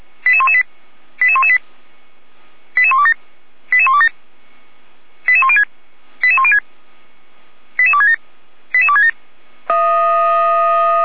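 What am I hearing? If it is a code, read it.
Five-tone selective call (ZVEI), heard as quick runs of five tones, tones of about 70 ms. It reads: 88118 88118 88127 88127 88106 88106 88267 88267
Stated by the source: Five-tone ZVEI selective call 88129, received on SDR, recorded by feuerwehr-orlamuende (fire brigade Orlamünde).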